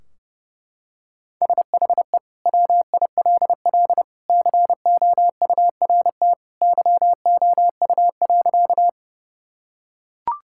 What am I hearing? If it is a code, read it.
Morse code, heard as a keyed tone, about 30 wpm, 700 Hz. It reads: SHE WILL COURT YOU.
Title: Sentences from Top 1600 Words 30wpm